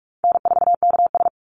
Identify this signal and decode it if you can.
Morse code, heard as a keyed tone, about 44 wpm, 700 Hz. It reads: N4KS
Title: Easy Call Signs Repeated 2x - 45wpm